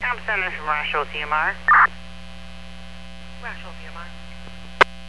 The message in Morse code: none

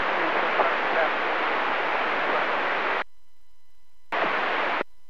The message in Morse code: none

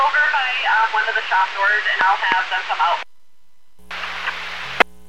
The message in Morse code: none

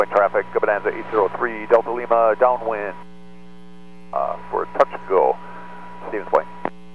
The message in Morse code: none

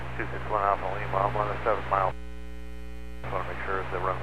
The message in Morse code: none

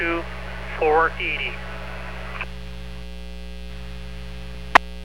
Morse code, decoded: none

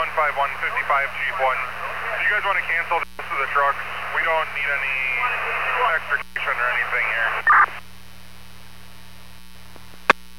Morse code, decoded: none

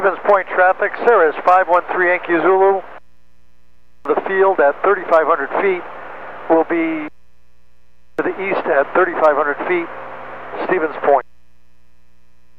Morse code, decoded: none

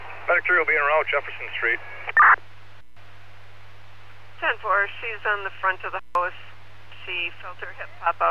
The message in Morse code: none

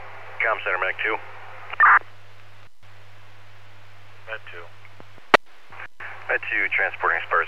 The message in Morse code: none

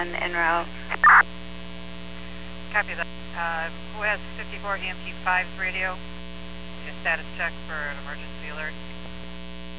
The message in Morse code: none

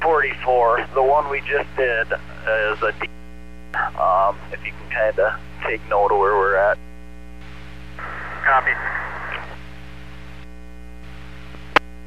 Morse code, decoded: none

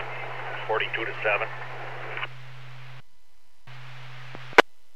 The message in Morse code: none

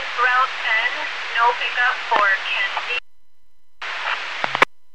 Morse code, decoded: none

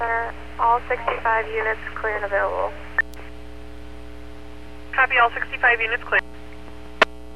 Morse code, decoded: none

